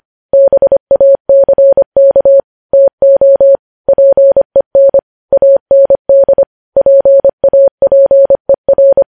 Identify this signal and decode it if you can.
Morse code, heard as a keyed tone, about 25 wpm, 550 Hz. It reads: BACK TO PEN AND PAPER
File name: podcast / News headlines-1x-25x25